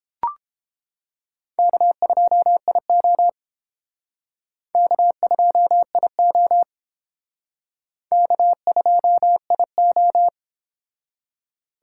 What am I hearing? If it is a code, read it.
Morse code, heard as a keyed tone, about 30 wpm, 700 Hz. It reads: K2IO K2IO K2IO